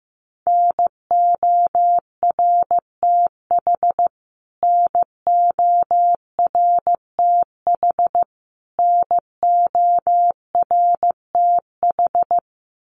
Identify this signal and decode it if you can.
Morse code, heard as a keyed tone, about 15 wpm, 700 Hz. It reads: NORTH NORTH NORTH
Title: Top 500 Words Repeated 15wpm